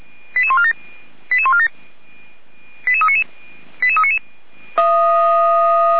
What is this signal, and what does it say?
Five-tone selective call (ZVEI), heard as quick runs of five tones, tones of about 70 ms. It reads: 88147 88147 88399 88399